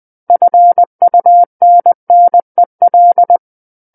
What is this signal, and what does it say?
Morse code, heard as a keyed tone, about 20 wpm, 700 Hz. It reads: FUNNEL
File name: 20 WPM